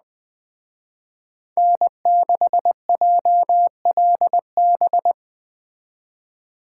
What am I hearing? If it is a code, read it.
Morse code, heard as a keyed tone, about 20 wpm, 700 Hz. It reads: N6JLB